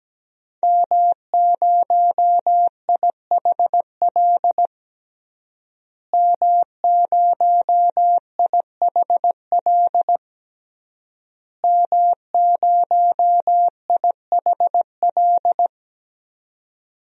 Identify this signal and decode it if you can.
Morse code, heard as a keyed tone, about 17 wpm, 700 Hz. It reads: M0IHL M0IHL M0IHL
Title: DXCC-100 Call Signs - 5 Characters - Repeated 3x - 17wpm